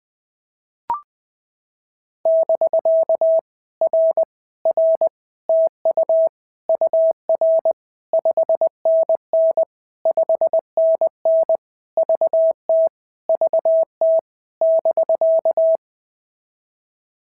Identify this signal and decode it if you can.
Morse code, heard as a keyed tone, about 20 wpm, 650 Hz. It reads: <BK> R R TU UR 5NN 5NN VT VT <BK>
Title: POTA QSO Practice - 20wpm